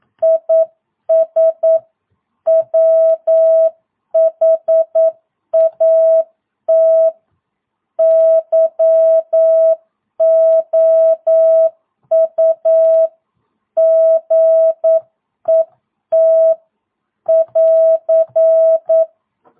Morse code, heard as a keyed tone, about 9 wpm, 650 Hz. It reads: IS WHAT YOU GET <AR>
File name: jh8cbh / 2022cw